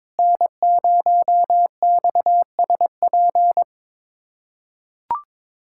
Morse code, heard as a keyed tone, about 22 wpm, 700 Hz. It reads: N0XSP
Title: USA Callsign DE Callsign - 5 Characters - 22wpm